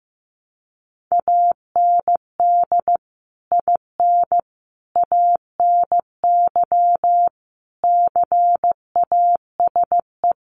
Morse code, heard as a keyed tone, about 15 wpm, 700 Hz. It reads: AND IN ANY CASE
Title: Sentences from Top 1200 Words 15wpm